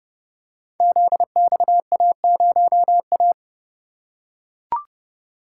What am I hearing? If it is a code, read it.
Morse code, heard as a keyed tone, about 30 wpm, 700 Hz. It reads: ZXA0A
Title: Code Groups - 5 Letter:Numbers - 30wpm